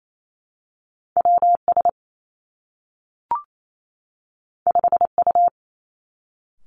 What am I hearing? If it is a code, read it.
Morse code, heard as a keyed tone, about 28 wpm, 700 Hz. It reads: W S 5 U
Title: Two Challenging Letter-Number Combinations 28wpm